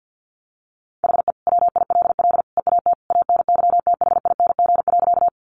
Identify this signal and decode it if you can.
Morse code, heard as a keyed tone, about 50 wpm, 700 Hz. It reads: HE WILL EAT ANYTHING.
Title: Sentences from Top 1000 Words 50wpm